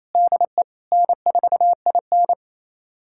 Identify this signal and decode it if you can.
Morse code, heard as a keyed tone, about 28 wpm, 700 Hz. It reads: DE N4IN
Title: USA Callsign DE Callsign - 4 Characters - 28wpm